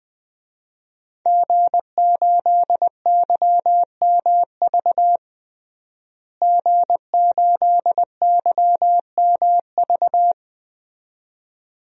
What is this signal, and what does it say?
Morse code, heard as a keyed tone, about 20 wpm, 700 Hz. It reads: G8YMV G8YMV
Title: DXCC-100 Call Signs - 5 Characters - Repeated 2x - 20wpm